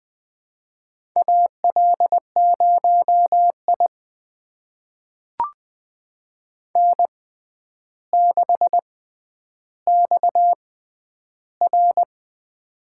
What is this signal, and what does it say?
Morse code, heard as a keyed tone, about 20 wpm, spaced as for 10 wpm, 700 Hz. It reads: AL0I N6XR
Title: USA Call Signs - 4 Characters - 6x Character Spacing - 20wpm